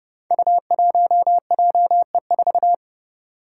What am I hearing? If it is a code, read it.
Morse code, heard as a keyed tone, about 30 wpm, 700 Hz. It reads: U1JE4
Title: Five Challenging Letter-Number Combinations 30wpm